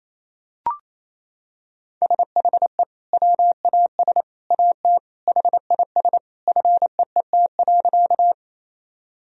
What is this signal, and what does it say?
Morse code, heard as a keyed tone, about 27 wpm, 700 Hz. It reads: SHE WAS AT HIS FEET.
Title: Sentences from Top 1000 Words - Review Part 1 - 28wpm